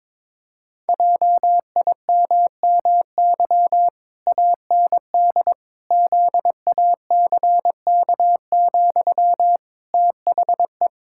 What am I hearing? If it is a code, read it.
Morse code, heard as a keyed tone, about 22 wpm, 700 Hz. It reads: JIMMY AND ZACK, THE
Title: Pangram Challenge v2 - 22wpm